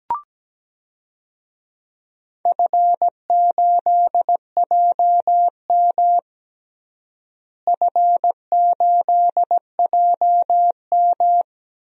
Morse code, heard as a keyed tone, about 17 wpm, 700 Hz. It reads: F8JM F8JM
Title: DXCC-100 Call Signs - 4 Characters - Repeated 2x - 17wpm